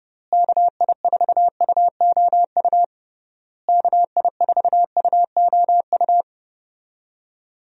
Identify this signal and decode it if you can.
Morse code, heard as a keyed tone, about 32 wpm, 700 Hz. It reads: KI4UOU KI4UOU
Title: USA Call Signs - 6 Characters - Repeated 3x - 30wpm